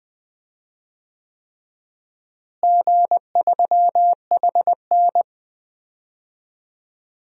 Morse code, heard as a keyed tone, about 20 wpm, 700 Hz. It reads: G3HN
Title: DXCC-100 Call Signs - 4 Characters - Repeated 3x - 20wpm